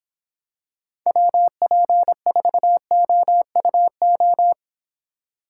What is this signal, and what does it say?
Morse code, heard as a keyed tone, about 26 wpm, 700 Hz. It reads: WP4OUO